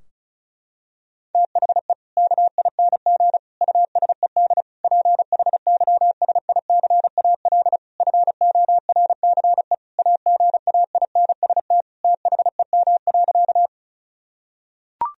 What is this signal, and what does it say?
Morse code, heard as a keyed tone, about 36 wpm, 700 Hz. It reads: THE KING USED PHYSICAL FORCE AGAINST THEM.
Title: Sentences from Top 1000 Words - Review Part 4 - 35wpm